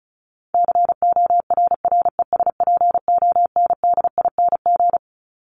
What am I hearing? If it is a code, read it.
Morse code, heard as a keyed tone, about 34 wpm, 700 Hz. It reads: CORRESPONDING
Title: ING Suffix - Top 100 Words with Base Words 35wpm